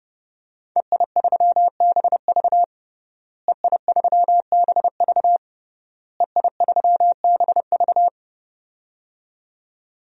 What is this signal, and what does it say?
Morse code, heard as a keyed tone, about 30 wpm, 700 Hz. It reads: EI3BV EI3BV EI3BV
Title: DXCC-100 Call Signs - 5 Characters - Repeated 3x - 30wpm